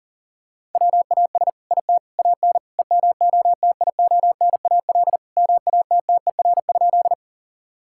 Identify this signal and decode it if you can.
Morse code, heard as a keyed tone, about 37 wpm, 700 Hz. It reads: WAS IT AN EMOTIONAL MATTER?
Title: Sentences from Top 1000 Words - Review Part 5 - 40wpm